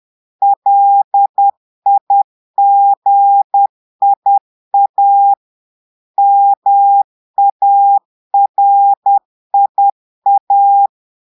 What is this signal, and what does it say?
Morse code, heard as a keyed tone, about 10 wpm, 800 Hz. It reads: LIGIA MARIA
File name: Portugais - texte en clair 1 - vitesse - 10 mots minute